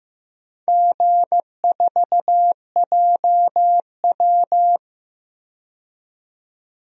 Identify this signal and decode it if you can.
Morse code, heard as a keyed tone, about 15 wpm, 700 Hz. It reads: G4JW
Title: DXCC-100 Call Signs - 4 Characters - 15wpm